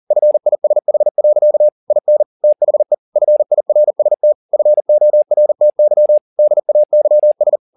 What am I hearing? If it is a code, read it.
Morse code, heard as a keyed tone, about 40 wpm, 600 Hz. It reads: FISH. IN THE FIRST FORTY DAYS
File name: the old man and the sea 1 40wpm